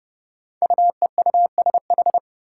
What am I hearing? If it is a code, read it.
Morse code, heard as a keyed tone, about 30 wpm, 700 Hz. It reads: UEUSH
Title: Five Challenging Letter-Number Combinations 30wpm